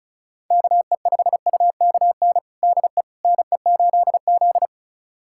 Morse code, heard as a keyed tone, about 36 wpm, 700 Hz. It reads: KE5UKN DE NE8Z
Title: USA Callsign DE Callsign 35wpm